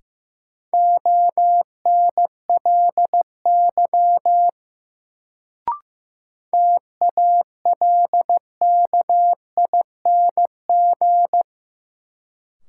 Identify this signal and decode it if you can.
Morse code, heard as a keyed tone, about 15 wpm, 700 Hz. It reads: ONLY TALKING
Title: Sets of 2 Words - New Word Drill - 15wpm